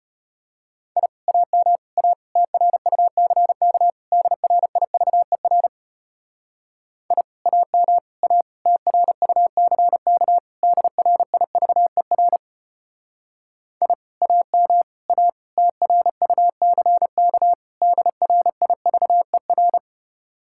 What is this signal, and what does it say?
Morse code, heard as a keyed tone, about 32 wpm, 700 Hz. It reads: I AM A TRUCK DRIVER I AM A TRUCK DRIVER I AM A TRUCK DRIVER